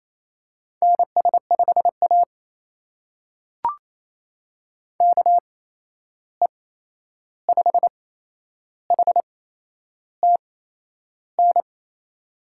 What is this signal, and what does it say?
Morse code, heard as a keyed tone, about 28 wpm, spaced as for 11 wpm, 700 Hz. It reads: NS5A KE5HTN
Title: USA Call Signs - 8x Character Spacing - 28wpm